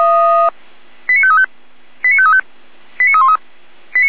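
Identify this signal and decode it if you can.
Five-tone selective call (ZVEI), heard as quick runs of five tones, tones of about 70 ms. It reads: 89535 89535 89313
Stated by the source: ZVEI five-tone selective call received with an SDR.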